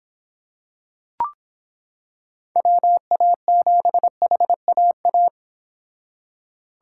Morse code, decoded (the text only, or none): WA7HAA